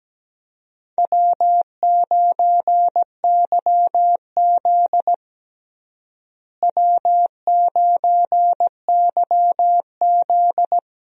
Morse code, heard as a keyed tone, about 17 wpm, 700 Hz. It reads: W9YZ W9YZ
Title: USA Call Signs - 4 Characters - Repeated 3x - 17wpm